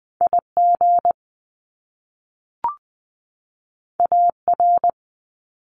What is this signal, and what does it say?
Morse code, heard as a keyed tone, about 20 wpm, 700 Hz. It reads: IG AR
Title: Top Two Letter Combinations in Words 20wpm